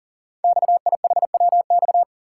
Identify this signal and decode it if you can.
Morse code, heard as a keyed tone, about 39 wpm, 690 Hz. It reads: XIHWX
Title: Code Groups - 5 Letters - 40wpm